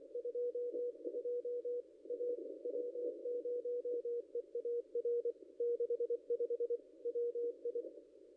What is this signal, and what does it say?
Morse code, heard as a keyed tone, about 24 wpm, 480 Hz. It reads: 22 A W0EAR 65 WI
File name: contests / ss-cw-k5tr-op-n5rz -0500